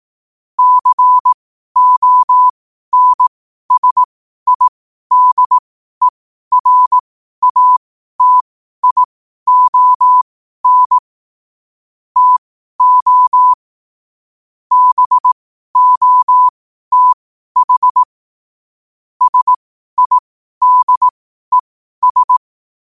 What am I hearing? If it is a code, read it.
Morse code, heard as a keyed tone, about 18 wpm, spaced as for 12 wpm, 1000 Hz. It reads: CONSIDERATION TO BOTH SIDES
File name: qotdfiles / 12wpm